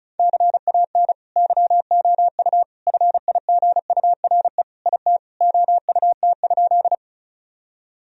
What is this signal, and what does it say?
Morse code, heard as a keyed tone, about 36 wpm, 700 Hz. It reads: CAN YOU FIGURE IT OUT?
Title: Sentences from Top 1000 Words - Review Part 2 - 35wpm